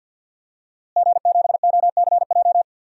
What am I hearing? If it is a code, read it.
Morse code, heard as a keyed tone, about 52 wpm, 700 Hz. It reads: G7OCJ
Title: DXCC-100 Call Signs - 5 Characters - Repeated 3x - 50wpm